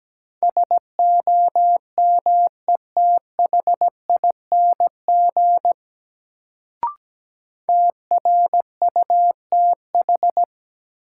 Morse code, heard as a keyed tone, about 17 wpm, 700 Hz. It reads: SOMETHING TRUTH
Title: Sets of 5 Words - New Word Drill - 17wpm